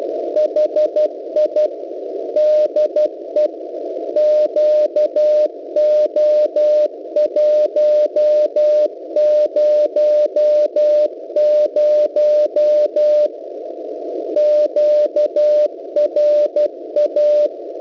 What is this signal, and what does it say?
Morse code, heard as a keyed tone, about 12 wpm, 600 Hz. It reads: HI DE QO100 QRA
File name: kiwisdr.local .3 cw